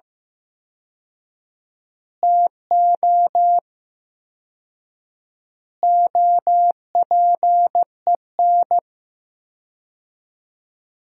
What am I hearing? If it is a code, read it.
Morse code, heard as a keyed tone, about 15 wpm, 700 Hz. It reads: TO OPEN
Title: 4x Word-Spacing - Sentences from Top 300 Words - 15wpm